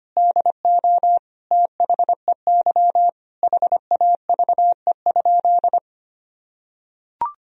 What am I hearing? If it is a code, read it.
Morse code, heard as a keyed tone, about 25 wpm, 700 Hz. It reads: DO THEY HAVE?
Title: Sentences from Top 1000 Words - Review Part 3 - 25wpm